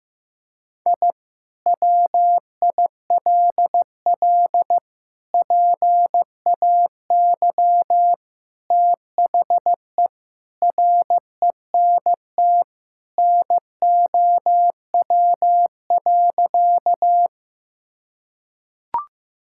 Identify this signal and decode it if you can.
Morse code, heard as a keyed tone, about 16 wpm, 700 Hz. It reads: I WILL PAY THE RENT NOW.
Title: Sentences from Top 1000 Words - Review Part 5 - 15wpm